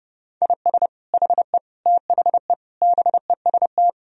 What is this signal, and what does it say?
Morse code, heard as a keyed tone, about 32 wpm, 700 Hz. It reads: IS HE THE BEST
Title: Sentences from Top 1000 Words - Review Part 4 - 30wpm